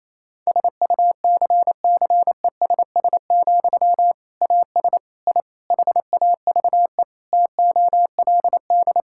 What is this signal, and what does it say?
Morse code, heard as a keyed tone, about 28 wpm, 700 Hz. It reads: SUCCESS, AS I HAVE TOLD
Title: Sentences from Top 1000 Words - Review Part 6 - 28wpm